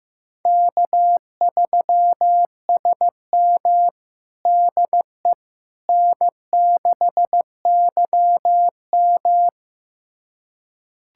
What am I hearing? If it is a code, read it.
Morse code, heard as a keyed tone, about 15 wpm, 700 Hz. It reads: K3SM DE N6YM
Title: USA Callsign DE Callsign - 4 Characters - 15wpm